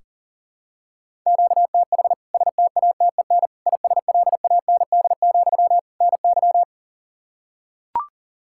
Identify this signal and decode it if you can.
Morse code, heard as a keyed tone, about 40 wpm, 700 Hz. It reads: QTH STATEN ISLAND, NY
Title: K1USN SST QSO Practice 40wpm